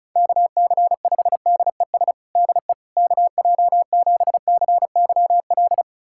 Morse code, heard as a keyed tone, about 34 wpm, 700 Hz. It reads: KC5DES DE KJ7CYL
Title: USA Callsign DE Callsign - 6 Characters - 35wpm